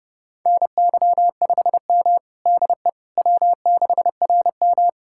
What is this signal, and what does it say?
Morse code, heard as a keyed tone, about 32 wpm, 700 Hz. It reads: NY5M DE W6RM
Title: USA Callsign DE Callsign - 4 Characters - 30wpm